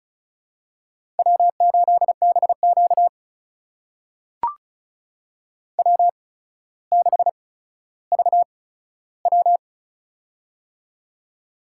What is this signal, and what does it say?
Morse code, heard as a keyed tone, about 36 wpm, spaced as for 15 wpm, 700 Hz. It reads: W8BQ W6VW
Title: USA Call Signs - 8x Character Spacing - 35wpm